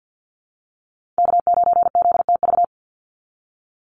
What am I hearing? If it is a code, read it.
Morse code, heard as a keyed tone, about 48 wpm, 700 Hz. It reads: K9ZTV
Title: Call Signs Repeated 2x - 50wpm